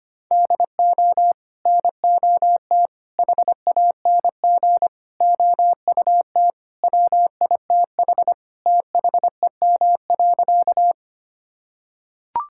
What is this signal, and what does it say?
Morse code, heard as a keyed tone, about 25 wpm, 700 Hz. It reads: DO NOT HANG OUT WITH THEM.
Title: Sentences from Top 1600 Words 25wpm